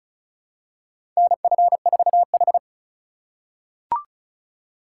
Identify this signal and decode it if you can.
Morse code, heard as a keyed tone, about 36 wpm, 700 Hz. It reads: NF4H
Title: USA Call Signs - 4 Characters - 3x Character Spacing - 35wpm